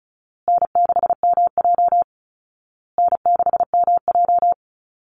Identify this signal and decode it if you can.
Morse code, heard as a keyed tone, about 34 wpm, 700 Hz. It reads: N6MJ N6MJ
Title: Easy Call Signs Repeated 3x - 35wpm